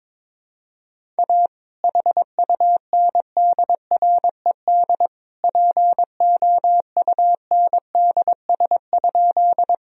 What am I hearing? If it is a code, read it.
Morse code, heard as a keyed tone, about 22 wpm, 700 Hz. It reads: A HUNDRED POUNDS?
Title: Sentences from Top 1000 Words - Review Part 2 - 22wpm